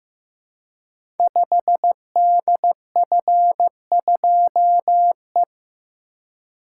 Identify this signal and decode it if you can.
Morse code, heard as a keyed tone, about 15 wpm, 700 Hz. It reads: 5DF2E
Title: Code Groups - 5 Letter:Numbers - 15wpm